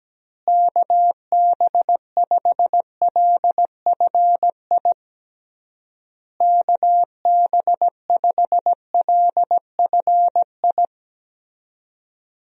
Morse code, heard as a keyed tone, about 17 wpm, 700 Hz. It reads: KB5LFI KB5LFI
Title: USA Call Signs - 6 Characters - Repeated 3x - 17wpm